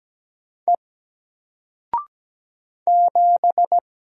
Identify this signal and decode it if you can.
Morse code, heard as a keyed tone, about 17 wpm, 700 Hz. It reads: E 7
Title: Single Letter-Number Alt 17wpm